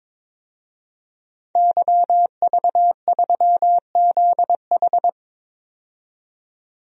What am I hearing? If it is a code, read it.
Morse code, heard as a keyed tone, about 22 wpm, 700 Hz. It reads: YV3ZH
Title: DXCC-100 Call Signs - 5 Characters - 22wpm